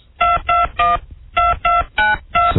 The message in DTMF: 2212292